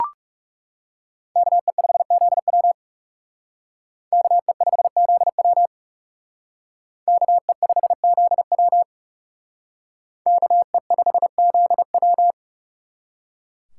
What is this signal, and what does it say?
Morse code, heard as a keyed tone, about 36 wpm, 700 Hz. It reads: KE5ZW KE5ZW KE5ZW KE5ZW
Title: Speed Racing - USA 2x2 Call Signs 30wpm